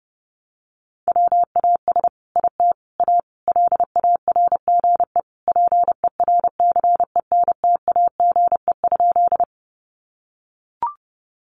Morse code, heard as a keyed tone, about 30 wpm, 700 Hz. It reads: WAS IT A LARGE PERCENTAGE?
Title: Sentences from Top 1300 Words 30wpm